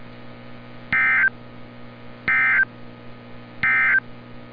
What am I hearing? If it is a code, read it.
Emergency Alert System (SAME) tones, heard as harsh buzzing bursts. It is heard from 0.8 to 4.1 s.
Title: NOAA Weather Radio - Altoona, PA